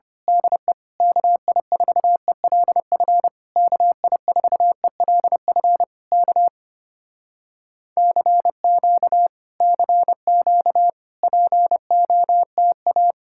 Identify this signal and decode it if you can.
Morse code, heard as a keyed tone, about 26 wpm, 700 Hz. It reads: DE KI4ELF KI4ELF K CQ CQ POTA
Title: Speed-Racing Common QSO Phrases - 25wpm